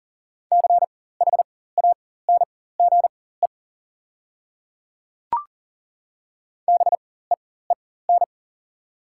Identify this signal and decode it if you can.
Morse code, heard as a keyed tone, about 40 wpm, spaced as for 19 wpm, 690 Hz. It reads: CHANGE BEEN
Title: 4x Character Spacing - Top 500 Words - 40wpm